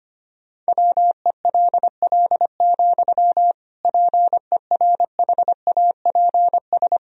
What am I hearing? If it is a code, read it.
Morse code, heard as a keyed tone, about 25 wpm, 700 Hz. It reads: WELL, PERHAPS
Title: Sentences from Top 1000 Words - Review Part 2 - 25wpm